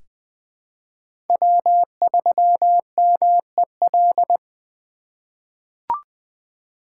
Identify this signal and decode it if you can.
Morse code, heard as a keyed tone, about 20 wpm, 700 Hz. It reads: W3MEL
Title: K1USN SST QSO Practice - 20wpm